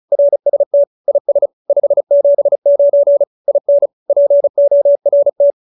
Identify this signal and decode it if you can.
Morse code, heard as a keyed tone, about 34 wpm, 570 Hz. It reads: RST IS 579 IN PORT
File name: QSO314 35